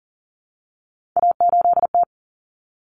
Morse code, heard as a keyed tone, about 40 wpm, 700 Hz. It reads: A8T
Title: Common Cut Numbers - 3-Digit 40wpm